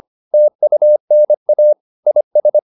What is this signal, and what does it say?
Morse code, heard as a keyed tone, about 25 wpm, 600 Hz. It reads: TUNA IS
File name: the old man and the sea part 1 25wpm